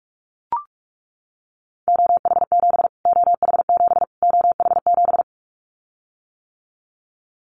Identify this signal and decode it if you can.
Morse code, heard as a keyed tone, about 44 wpm, 700 Hz. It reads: OH7 OH7 OH7
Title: Callsign Prefixes for Top 25 DXCC - Repeated 3x - 45wpm